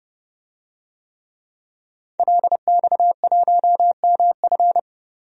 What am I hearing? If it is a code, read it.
Morse code, heard as a keyed tone, about 30 wpm, 700 Hz. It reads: LX1MF
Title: DXCC-100 Call Signs - 5 Characters - Repeated 2x - 30wpm